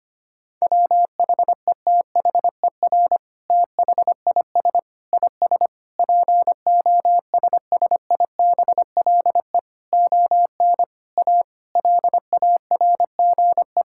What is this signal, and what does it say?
Morse code, heard as a keyed tone, about 25 wpm, 700 Hz. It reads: WHETHER THIS IS POSSIBLE ON A LARGE
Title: Sentences from Top 1000 Words - Review Part 5 - 25wpm